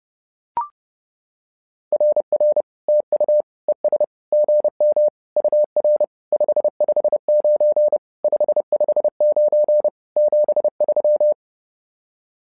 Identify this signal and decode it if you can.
Morse code, heard as a keyed tone, about 30 wpm, 600 Hz. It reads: RR TU ES GM UR 559 559 73
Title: SOTA QSO Practice 30wpm